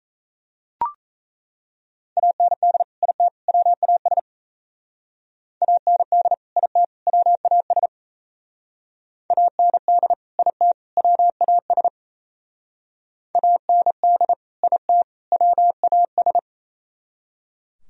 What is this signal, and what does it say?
Morse code, heard as a keyed tone, about 33 wpm, 700 Hz. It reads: AND IT WAS AND IT WAS AND IT WAS AND IT WAS
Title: Speed-Racing - Sets of 3 Words 28wpm